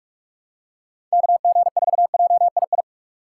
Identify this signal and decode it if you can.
Morse code, heard as a keyed tone, about 42 wpm, 700 Hz. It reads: KG4JII
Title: USA Call Signs - 6 Characters - 6x Character Spacing - 45wpm